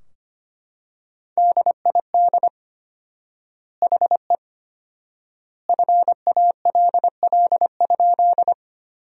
Morse code, heard as a keyed tone, about 25 wpm, 700 Hz. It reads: DID HE FALL?
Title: 4x Word-Spacing - Sentences from Top 500 Words - 25wpm